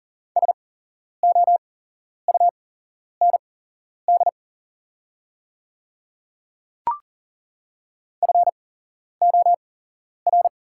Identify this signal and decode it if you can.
Morse code, heard as a keyed tone, about 42 wpm, spaced as for 11 wpm, 690 Hz. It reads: SOUND FOR
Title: 8x Character Spacing - Top 100 Words - 40wpm